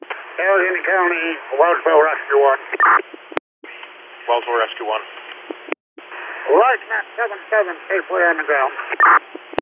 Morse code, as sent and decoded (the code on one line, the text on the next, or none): none